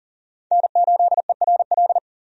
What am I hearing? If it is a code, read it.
Morse code, heard as a keyed tone, about 40 wpm, 700 Hz. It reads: N8ERL
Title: USA Call Signs - 5 Characters - 40wpm